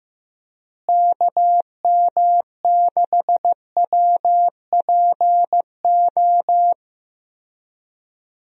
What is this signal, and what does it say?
Morse code, heard as a keyed tone, about 15 wpm, 700 Hz. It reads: KM6WPO
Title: USA Call Signs - 6 Characters - Repeated 2x - 15wpm